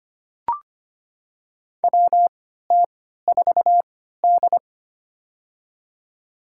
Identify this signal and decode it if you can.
Morse code, heard as a keyed tone, about 25 wpm, spaced as for 14 wpm, 700 Hz. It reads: WT4D
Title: USA Call Signs - 4 Characters - 3x Character Spacing - 25wpm